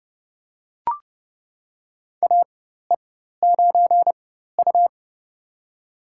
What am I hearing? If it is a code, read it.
Morse code, heard as a keyed tone, about 31 wpm, spaced as for 14 wpm, 700 Hz. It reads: AE9U
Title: USA Call Signs - 4x Character Spacing - 30wpm